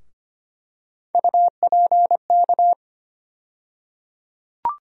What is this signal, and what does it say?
Morse code, heard as a keyed tone, about 25 wpm, 700 Hz. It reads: UPK
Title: USA Call Sign Postfixes - 3 Letters 25wpm